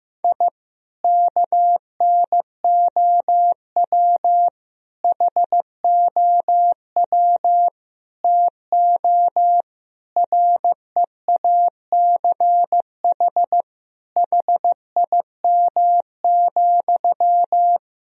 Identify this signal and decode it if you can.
Morse code, heard as a keyed tone, about 15 wpm, 700 Hz. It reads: I KNOW HOW TO REACH HIM,
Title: Sentences from Top 1000 Words - Review Part 2 - 15wpm